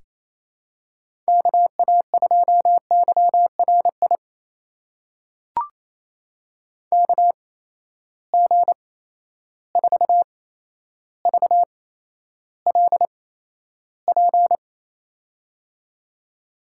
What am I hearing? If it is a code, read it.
Morse code, heard as a keyed tone, about 28 wpm, spaced as for 12 wpm, 700 Hz. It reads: KA2YRI KG4VLP
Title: USA Call Signs - 6 Characters - 8x Character Spacing - 28wpm